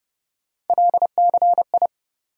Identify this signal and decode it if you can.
Morse code, heard as a keyed tone, about 30 wpm, 700 Hz. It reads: LCI
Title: USA Call Sign Postfixes - 3 Letters 30wpm